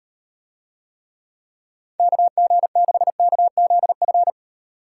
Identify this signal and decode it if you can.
Morse code, heard as a keyed tone, about 36 wpm, 700 Hz. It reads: KG6KZF